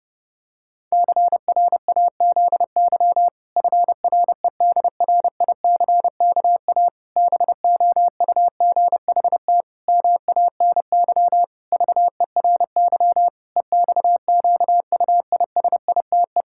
Morse code, heard as a keyed tone, about 30 wpm, 700 Hz. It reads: CRAZY FREDRICKA BOUGHT MANY VERY EXQUISITE
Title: Pangram Challenge v2 - 30wpm